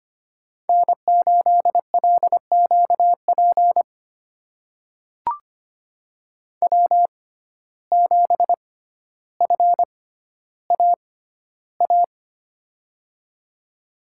Morse code, heard as a keyed tone, about 25 wpm, spaced as for 13 wpm, 700 Hz. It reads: N8LQP W7FAA
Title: USA Call Signs - 5 Characters - 6x Character Spacing - 25wpm